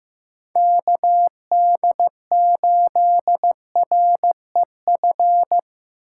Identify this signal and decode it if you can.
Morse code, heard as a keyed tone, about 17 wpm, 700 Hz. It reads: KD8REF
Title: Speed Racing - USA 2x3 Call Signs 15wpm